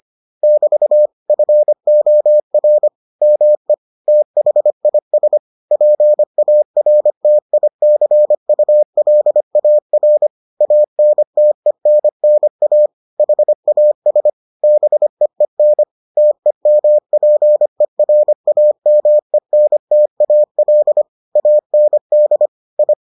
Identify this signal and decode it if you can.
Morse code, heard as a keyed tone, about 25 wpm, 600 Hz. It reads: <BT> FOR ME THIS PARTICULAR ANTENNA HAS BEEN TEMPERAMENTAL AND I